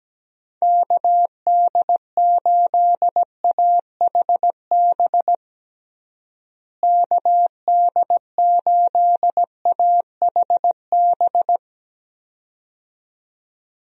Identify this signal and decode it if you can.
Morse code, heard as a keyed tone, about 17 wpm, 700 Hz. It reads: KD8AHB KD8AHB
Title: USA Call Signs - 6 Characters - Repeated 2x - 17wpm